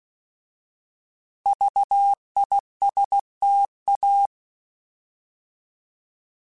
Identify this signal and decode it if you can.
Morse code, heard as a keyed tone, about 16 wpm, 800 Hz. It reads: VISTA